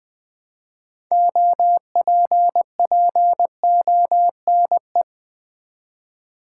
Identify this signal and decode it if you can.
Morse code, heard as a keyed tone, about 20 wpm, 700 Hz. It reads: OPPONE
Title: Build A Word - 2 Letters At A Time - 8-Letter Words - 20wpm